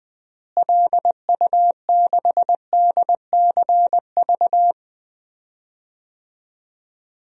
Morse code, heard as a keyed tone, about 20 wpm, 700 Hz. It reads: LU6DCV
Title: DXCC-100 Call Signs - 6 Characters - Repeated 2x - 20wpm